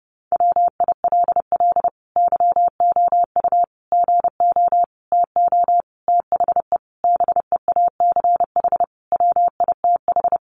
Morse code, heard as a keyed tone, about 30 wpm, 700 Hz. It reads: WILL YOU GO TO THE BEACH WITH
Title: Sentences from Top 1500 Words 30wpm